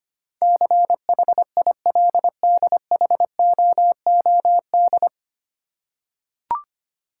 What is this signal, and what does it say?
Morse code, heard as a keyed tone, about 25 wpm, 700 Hz. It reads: CHILDHOOD
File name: practice / 25wpm / 9 Letter Words - First 3 Letters Given - 25wpm